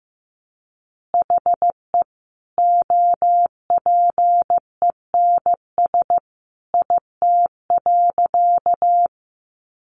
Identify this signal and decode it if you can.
Morse code, heard as a keyed tone, about 15 wpm, 700 Hz. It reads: HE OPENS IT.
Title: Additional Easy Phrases 15wpm